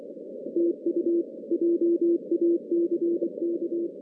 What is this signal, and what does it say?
Morse code, heard as a keyed tone, about 25 wpm, 340 Hz. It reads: TU JACK
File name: 2016 JAN CW NAQP / 2016 Jan naqp k5ot - 0-10